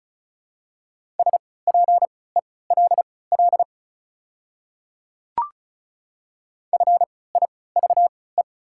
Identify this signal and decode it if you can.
Morse code, heard as a keyed tone, about 34 wpm, spaced as for 20 wpm, 700 Hz. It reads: SPELL FIVE
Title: 3x Character Spacing - Top 500 Words - 35wpm